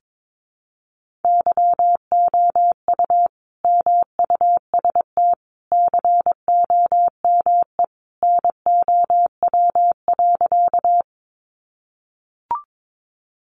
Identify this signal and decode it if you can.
Morse code, heard as a keyed tone, about 22 wpm, 700 Hz. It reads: YOU MUST COME NOW.
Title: Sentences from Top 200 Words 22wpm